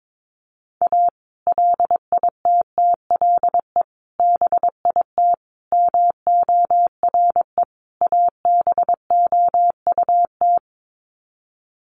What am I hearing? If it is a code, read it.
Morse code, heard as a keyed tone, about 22 wpm, 700 Hz. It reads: A LITTLE BIT MORE ABOUT
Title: Sets of 5 Words Encore2 22wpm